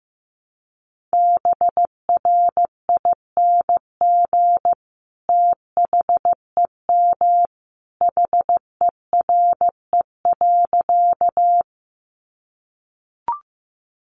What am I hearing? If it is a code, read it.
Morse code, heard as a keyed tone, about 15 wpm, 700 Hz. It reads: BRING THEM HERE.
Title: Sentences from Top 500 Words 15wpm